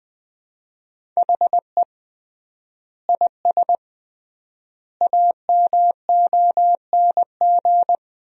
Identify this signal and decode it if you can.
Morse code, heard as a keyed tone, about 21 wpm, 700 Hz. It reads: HE IS AMONG
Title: 3x Word-Spacing - Sentences from Top 500 Words - 20wpm